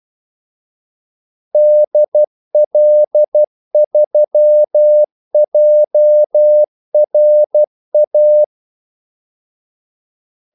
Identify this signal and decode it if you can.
Morse code, heard as a keyed tone, about 12 wpm, 600 Hz. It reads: DL3JRA
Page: 2 ekstra ordmellemrum mellem kaldesignalerne